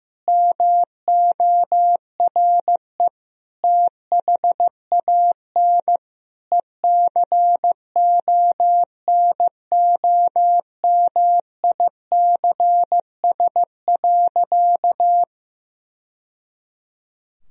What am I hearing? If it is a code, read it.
Morse code, heard as a keyed tone, about 14 wpm, 700 Hz. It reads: MORE THAN ECONOMICS.
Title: Sentences from Top 1000 Words - Review Part 3 - 15wpm